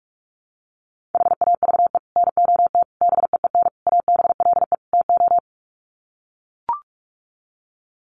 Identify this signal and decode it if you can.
Morse code, heard as a keyed tone, about 44 wpm, 700 Hz. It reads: HAVE NOT BEEN ABLE TO
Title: Sets of 5 Words 45wpm